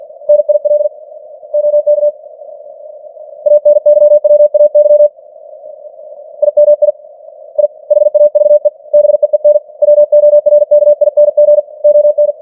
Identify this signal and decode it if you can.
Morse code, heard as a keyed tone, about 50 wpm, 590 Hz. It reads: NIL OK ANYWAY IMI I HAVE BEEN WORKING ON
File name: archived / CWBros